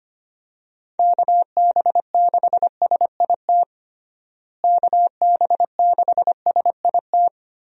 Morse code, heard as a keyed tone, about 25 wpm, 700 Hz. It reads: KB6SIT KB6SIT
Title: USA Call Signs - 6 Characters - Repeated 3x - 25wpm